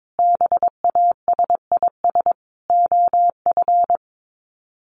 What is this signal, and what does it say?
Morse code, heard as a keyed tone, about 22 wpm, 700 Hz. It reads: BASIS OF
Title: Sets of 3 Words 22wpm